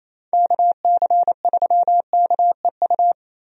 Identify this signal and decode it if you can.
Morse code, heard as a keyed tone, about 28 wpm, 700 Hz. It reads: KC3KEU
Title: USA Call Signs - 6 Characters - Repeated 3x - 28wpm